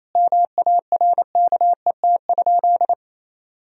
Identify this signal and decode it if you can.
Morse code, heard as a keyed tone, about 28 wpm, 700 Hz. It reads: MARKET?
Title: Sentences from Top 1000 Words - Review Part 3 - 28wpm